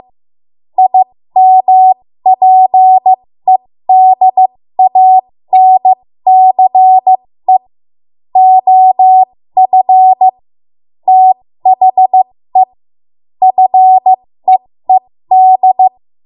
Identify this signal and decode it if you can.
Morse code, heard as a keyed tone, about 15 wpm, spaced as for 13 wpm, 750 Hz. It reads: IMPEDANCE OF THE FEED
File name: Morse / 13WPM